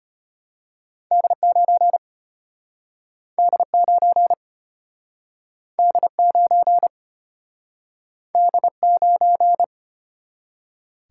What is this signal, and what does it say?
Morse code, heard as a keyed tone, about 34 wpm, 700 Hz. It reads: D9 D9 D9 D9